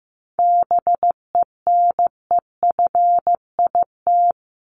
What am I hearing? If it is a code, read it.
Morse code, heard as a keyed tone, about 15 wpm, 700 Hz. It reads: BENEFIT
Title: Top 751-1000 Words 15wpm